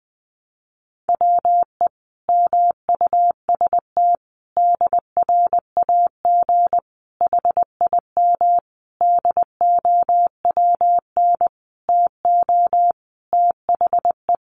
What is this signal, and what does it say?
Morse code, heard as a keyed tone, about 20 wpm, 700 Hz. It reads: WE MUST DRAG HIM DOWN TO THE
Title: Sentences from Top 1500 Words 20wpm